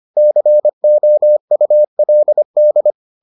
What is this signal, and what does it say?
Morse code, heard as a keyed tone, about 25 wpm, 600 Hz. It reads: COULD